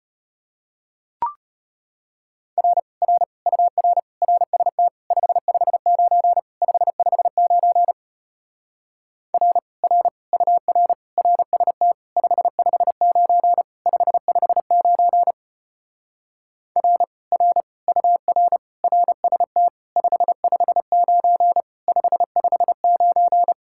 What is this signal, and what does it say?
Morse code, heard as a keyed tone, about 33 wpm, 700 Hz. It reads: R R UR RST 559 559 R R UR RST 559 559 R R UR RST 559 559